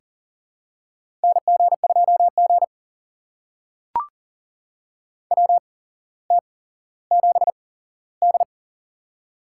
Morse code, heard as a keyed tone, about 39 wpm, spaced as for 17 wpm, 700 Hz. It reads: NG2G WT7D
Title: USA Call Signs - 4 Characters - 8x Character Spacing - 40wpm